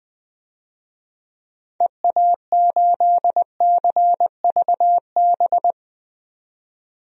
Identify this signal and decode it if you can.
Morse code, heard as a keyed tone, about 20 wpm, 700 Hz. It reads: EA8CVB
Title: DXCC-100 Call Signs - 6 Characters - 20wpm